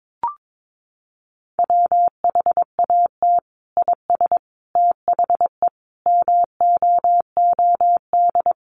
Morse code, heard as a keyed tone, about 22 wpm, 700 Hz. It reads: WHAT IS THE MOOD
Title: Sentences from Top 1100 Words 22wpm